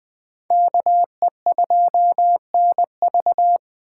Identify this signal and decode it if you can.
Morse code, heard as a keyed tone, about 20 wpm, 700 Hz. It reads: KE2NV
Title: Speed Racing - USA Call Signs 17wpm